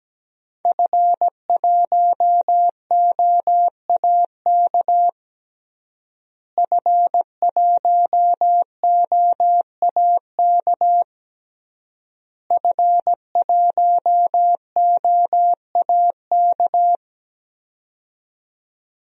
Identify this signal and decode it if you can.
Morse code, heard as a keyed tone, about 17 wpm, 700 Hz. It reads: F1OAK F1OAK F1OAK
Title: DXCC-100 Call Signs - 5 Characters - Repeated 3x - 17wpm